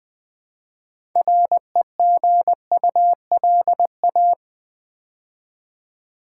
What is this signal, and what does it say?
Morse code, heard as a keyed tone, about 20 wpm, 700 Hz. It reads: REGULA
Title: Build A Word - 3 Letters At A Time - 9-Letter Words - 20wpm